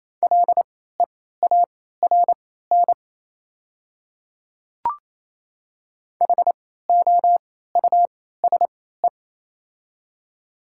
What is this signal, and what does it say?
Morse code, heard as a keyed tone, about 28 wpm, spaced as for 16 wpm, 700 Hz. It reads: LEARN HOUSE
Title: 3x Character Spacing - Top 500 Words - 28wpm